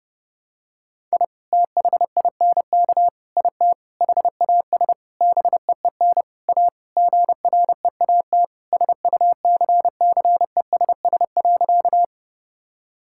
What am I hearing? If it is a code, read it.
Morse code, heard as a keyed tone, about 30 wpm, 700 Hz. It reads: I THINK IT HAS BEEN A GREAT SUCCESS.
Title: Sentences from Top 1000 Words - Review Part 6 - 30wpm